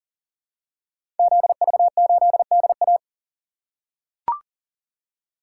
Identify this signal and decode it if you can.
Morse code, heard as a keyed tone, about 38 wpm, 700 Hz. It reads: ZV8DA
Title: Code Groups - 5 Letter:Numbers - 40wpm